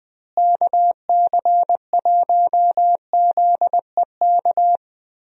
Morse code, heard as a keyed tone, about 20 wpm, 700 Hz. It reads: KC1ZEK
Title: USA Call Signs - 6 Characters - Repeated 3x - 20wpm